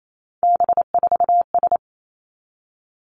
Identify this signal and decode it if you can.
Morse code, heard as a keyed tone, about 28 wpm, 700 Hz. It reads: B4S
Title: Three Challenging Letter-Number Combinations 28wpm